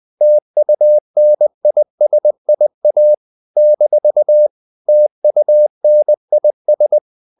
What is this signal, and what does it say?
Morse code, heard as a keyed tone, about 20 wpm, 600 Hz. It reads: TUNISIA - TUNIS
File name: CW / Practice / NationalCapitals20